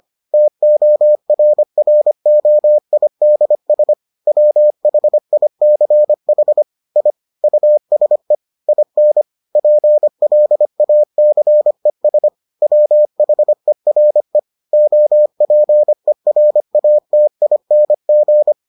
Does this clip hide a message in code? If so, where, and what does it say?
Yes: Morse code from 0.3 to 18.5 s, heard as a keyed tone, about 25 wpm, 600 Hz. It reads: TORROIDS WHICH I USE IN PLACES WHERE OPERATING